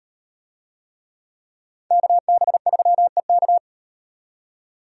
Morse code, heard as a keyed tone, about 38 wpm, 700 Hz. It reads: KB3EK